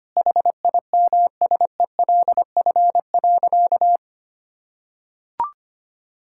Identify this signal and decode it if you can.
Morse code, heard as a keyed tone, about 25 wpm, 700 Hz. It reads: HIMSELF.
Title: Sentences from Top 1600 Words 25wpm